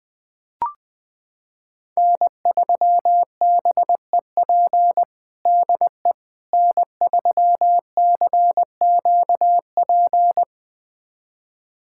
Morse code, heard as a keyed tone, about 20 wpm, 700 Hz. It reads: N3BEP DE N3CQP
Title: USA Callsign DE Callsign - 5 Characters - 20wpm